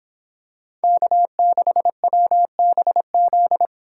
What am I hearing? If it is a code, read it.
Morse code, heard as a keyed tone, about 26 wpm, 700 Hz. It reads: K6WBZ